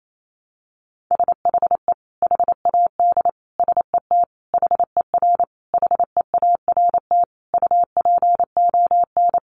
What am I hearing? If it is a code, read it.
Morse code, heard as a keyed tone, about 28 wpm, 700 Hz. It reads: SHE HAD SET HER HEART UPON
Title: Sentences from Top 1200 Words 28wpm